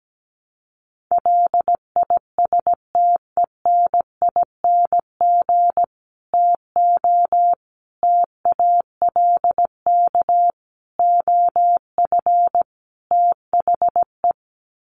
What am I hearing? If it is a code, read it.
Morse code, heard as a keyed tone, about 17 wpm, 700 Hz. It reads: LISTENING TO TALK OF THE
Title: Sets of 5 Words 17wpm